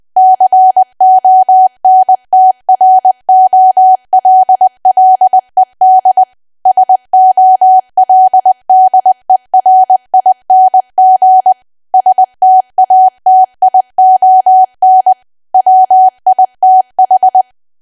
Morse code, heard as a keyed tone, about 20 wpm, 750 Hz. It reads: CONTROLLED SOLDERING STATION WITH
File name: cwa / W1AW-Level3 / 20WPM